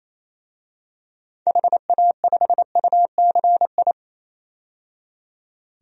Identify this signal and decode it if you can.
Morse code, heard as a keyed tone, about 27 wpm, 700 Hz. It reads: HA5UCI
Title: DXCC-100 Call Signs - 6 Characters - 28wpm